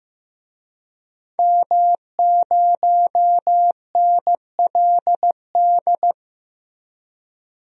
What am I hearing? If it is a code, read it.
Morse code, heard as a keyed tone, about 15 wpm, 700 Hz. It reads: M0NLD
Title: DXCC-100 Call Signs - 5 Characters - Repeated 2x - 15wpm